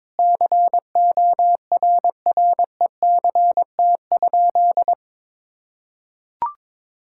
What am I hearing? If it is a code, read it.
Morse code, heard as a keyed tone, about 22 wpm, 700 Hz. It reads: CORRECT?
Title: Sentences from Top 1000 Words - Review Part 6 - 22wpm